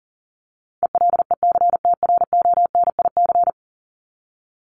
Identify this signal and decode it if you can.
Morse code, heard as a keyed tone, about 40 wpm, 700 Hz. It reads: ELECTRONIC
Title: Top 1001-1250 Words 40wpm